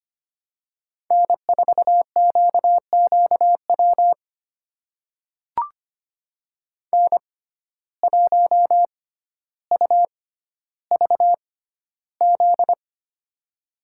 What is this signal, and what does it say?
Morse code, heard as a keyed tone, about 25 wpm, spaced as for 13 wpm, 700 Hz. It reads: N4QQW N1UVZ
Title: USA Call Signs - 5 Characters - 6x Character Spacing - 25wpm